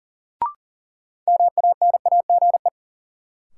Morse code, heard as a keyed tone, about 39 wpm, 700 Hz. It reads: MANAGE
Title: Sentences from Top 1400 Words - New Word Drill - 40wpm